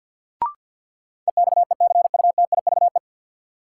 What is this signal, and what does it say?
Morse code, heard as a keyed tone, about 52 wpm, 700 Hz. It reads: EXECUTIVE
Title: Sets of 5 Words Encore - New Word Drill - 50wpm